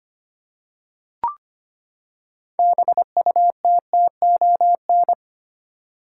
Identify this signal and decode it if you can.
Morse code, heard as a keyed tone, about 25 wpm, 700 Hz. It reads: BUTTON
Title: Six Letter Words - 25wpm